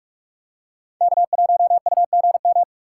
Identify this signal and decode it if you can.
Morse code, heard as a keyed tone, about 41 wpm, 700 Hz. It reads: K1UGM
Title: USA Call Signs - 6x Character Spacing - 45wpm